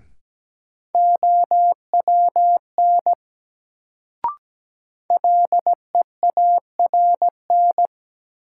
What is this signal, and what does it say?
Morse code, heard as a keyed tone, about 17 wpm, 700 Hz. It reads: OWN LEARN
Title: Sentences from Top 300 Words - New Word Drill - 17wpm